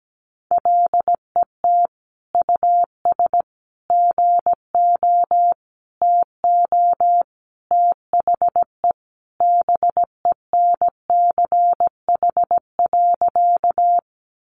Morse code, heard as a keyed tone, about 17 wpm, 700 Hz. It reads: LET US GO TO THE BENCH.
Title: Sentences from Top 1400 Words 17wpm